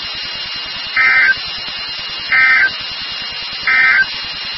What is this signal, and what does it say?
Emergency Alert System (SAME) tones, heard as harsh buzzing bursts. It is heard from 0.8 to 4.1 s.
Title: NOAA Weather Radio WXK36